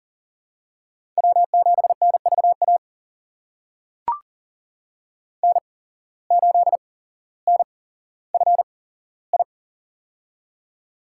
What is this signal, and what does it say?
Morse code, heard as a keyed tone, about 39 wpm, spaced as for 17 wpm, 700 Hz. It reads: W7NVA N8NFI
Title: USA Call Signs - 5 Characters - 8x Character Spacing - 40wpm